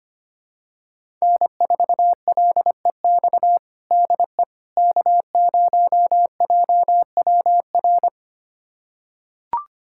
Morse code, heard as a keyed tone, about 25 wpm, 700 Hz. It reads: N4LEX DE K0JWR
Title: USA Callsign DE Callsign - 5 Characters - 25wpm